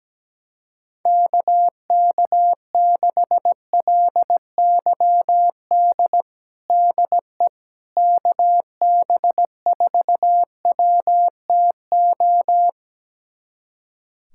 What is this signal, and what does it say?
Morse code, heard as a keyed tone, about 17 wpm, 700 Hz. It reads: KK6LYD DE KB4WTO
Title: USA Callsign DE Callsign - 6 Characters - 17wpm